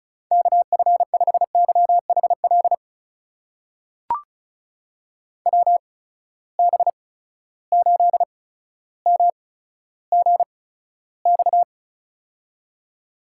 Morse code, heard as a keyed tone, about 36 wpm, spaced as for 15 wpm, 700 Hz. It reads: KF5YHL WB8MGX
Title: USA Call Signs - 6 Characters - 8x Character Spacing - 35wpm